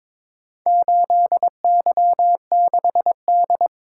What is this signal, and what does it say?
Morse code, heard as a keyed tone, about 22 wpm, 700 Hz. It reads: 8Y6D
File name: practice / 22wpm / Four Letter-Number Combinations - 22wpm